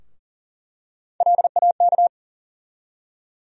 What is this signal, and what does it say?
Morse code, heard as a keyed tone, about 40 wpm, 700 Hz. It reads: LAK
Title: USA Call Sign Postfixes - 3 Letters 40wpm